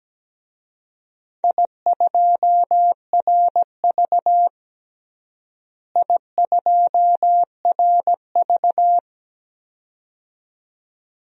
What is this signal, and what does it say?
Morse code, heard as a keyed tone, about 17 wpm, 700 Hz. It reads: I2RV I2RV
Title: DXCC-100 Call Signs - 4 Characters - Repeated 2x - 17wpm